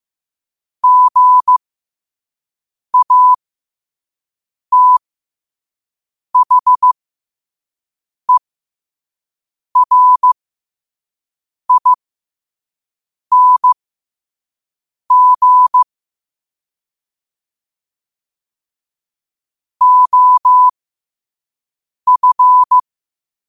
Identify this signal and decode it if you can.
Morse code, heard as a keyed tone, about 15 wpm, spaced as for 5 wpm, 1000 Hz. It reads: GATHERING OF
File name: qotdfiles / 5wpm